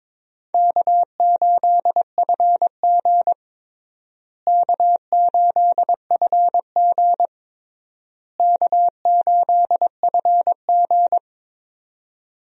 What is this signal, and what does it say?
Morse code, heard as a keyed tone, about 22 wpm, 700 Hz. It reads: K8FG K8FG K8FG
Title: USA Call Signs - 4 Characters - Repeated 3x - 22wpm